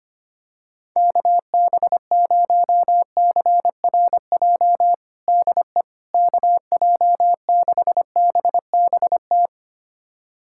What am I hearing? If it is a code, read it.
Morse code, heard as a keyed tone, about 25 wpm, 700 Hz. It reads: KB0CRJ DE KJ6BBT
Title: USA Callsign DE Callsign - 6 Characters - 25wpm